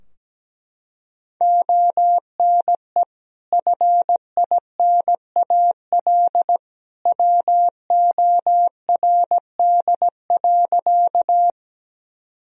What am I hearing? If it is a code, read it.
Morse code, heard as a keyed tone, about 17 wpm, 700 Hz. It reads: ONE FINAL WORD.
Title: Sentences from Top 1000 Words - Review Part 2 - 17wpm